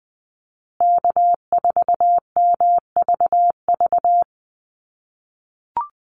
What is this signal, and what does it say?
Morse code, heard as a keyed tone, about 20 wpm, 700 Hz. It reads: K4MVV
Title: USA Call Signs Repeated 3x 20wpm